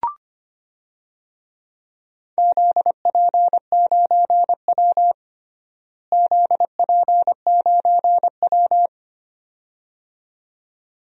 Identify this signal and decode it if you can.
Morse code, heard as a keyed tone, about 25 wpm, 700 Hz. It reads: ZP9W ZP9W
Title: DXCC-100 Call Signs - 4 Characters - Repeated 2x - 25wpm